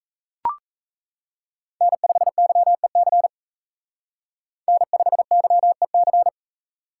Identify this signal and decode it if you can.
Morse code, heard as a keyed tone, about 35 wpm, 700 Hz. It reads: N5YEC N5YEC
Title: Speed Racing - USA 1x3 Call Signs 28wpm